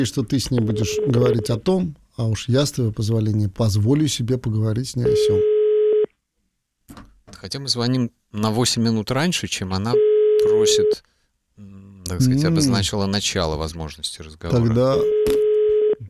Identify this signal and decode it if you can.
Ringback tone (Europe, 425 Hz), with speech.